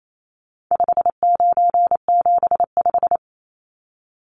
Morse code, heard as a keyed tone, about 28 wpm, 700 Hz. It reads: 5975
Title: Numbers - 4-Digit 28wpm